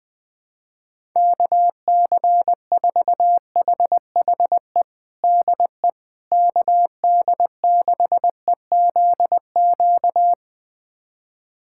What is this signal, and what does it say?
Morse code, heard as a keyed tone, about 21 wpm, 700 Hz. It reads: KC4HHE DE KD6EZQ
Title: USA Callsign DE Callsign - 6 Characters - 20wpm